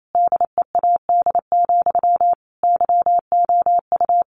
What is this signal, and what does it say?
Morse code, heard as a keyed tone, about 28 wpm, 700 Hz. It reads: DEAD, YOU
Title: A Sound Of Thunder 28wpm